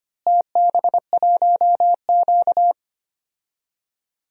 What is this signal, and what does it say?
Morse code, heard as a keyed tone, about 25 wpm, 700 Hz. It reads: TB1Q
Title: DXCC-100 Call Signs - 4 Characters - Repeated 3x - 25wpm